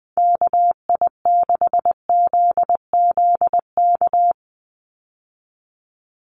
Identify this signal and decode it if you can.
Morse code, heard as a keyed tone, about 20 wpm, 700 Hz. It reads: KI6ZZK
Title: USA Call Signs 20wpm